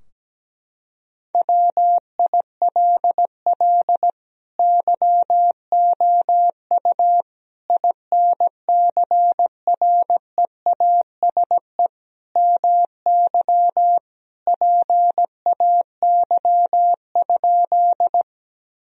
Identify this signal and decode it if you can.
Morse code, heard as a keyed tone, about 17 wpm, 700 Hz. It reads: WILL YOU INCREASE MY PAY?
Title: Sentences from Top 1000 Words - Review Part 4 - 17wpm